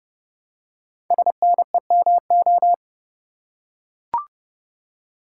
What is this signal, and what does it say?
Morse code, heard as a keyed tone, about 30 wpm, 700 Hz. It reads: SNEMO
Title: Code Groups - 5 Letters - 30wpm